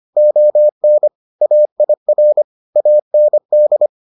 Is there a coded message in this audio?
Yes: Morse code from 0.2 to 3.9 s, heard as a keyed tone, about 25 wpm, 600 Hz. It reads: ON AIR AND